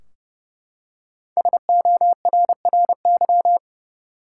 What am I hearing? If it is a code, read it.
Morse code, heard as a keyed tone, about 32 wpm, 700 Hz. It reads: SORRY
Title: Five Letter Words - 30wpm